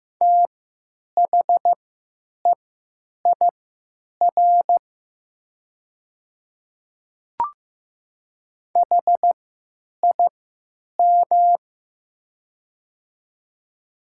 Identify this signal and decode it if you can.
Morse code, heard as a keyed tone, about 15 wpm, spaced as for 9 wpm, 700 Hz. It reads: THEIR HIM
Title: 3x Character Spacing - Top 500 Words - 15wpm